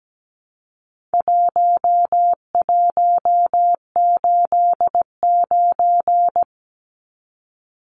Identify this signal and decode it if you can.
Morse code, heard as a keyed tone, about 17 wpm, 700 Hz. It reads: 1189
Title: Numbers - 4-Digit 17wpm